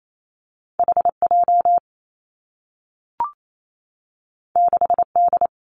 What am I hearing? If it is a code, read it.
Morse code, heard as a keyed tone, about 28 wpm, 700 Hz. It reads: H J 6 D
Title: Two Challenging Letter-Number Combinations 28wpm